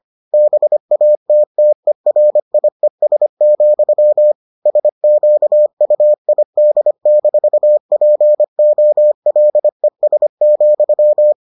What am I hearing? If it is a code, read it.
Morse code, heard as a keyed tone, about 25 wpm, 600 Hz. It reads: BATTERIES, SQUID-POLES,